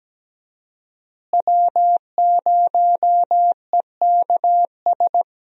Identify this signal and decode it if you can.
Morse code, heard as a keyed tone, about 17 wpm, 700 Hz. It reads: W0EKS
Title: USA Call Signs - 5 Characters - Repeated 3x - 17wpm